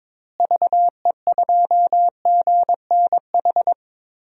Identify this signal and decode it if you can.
Morse code, heard as a keyed tone, about 22 wpm, 700 Hz. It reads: VE2GNH
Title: DXCC-100 Call Signs - 6 Characters - Repeated 3x - 22wpm